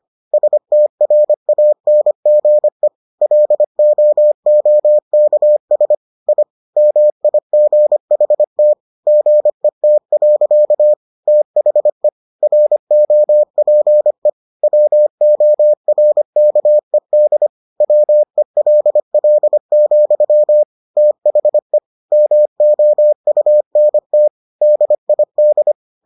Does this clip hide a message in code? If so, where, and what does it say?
Yes: Morse code from 0.3 to 25.7 s, heard as a keyed tone, about 25 wpm, 600 Hz. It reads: STRANGE LOOKS I MIGHT GET. THE ROPE WORKED WELL, THE MOUNT DID